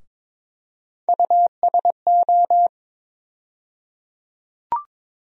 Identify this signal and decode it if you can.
Morse code, heard as a keyed tone, about 22 wpm, 700 Hz. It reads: USO
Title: USA Call Sign Postfixes - 3 Letters 22wpm